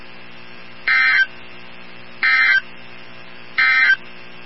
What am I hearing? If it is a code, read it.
Emergency Alert System (SAME) tones, heard as harsh buzzing bursts. It is heard from 0.8 to 4.0 s.
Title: NOAA Weather Radio 162.550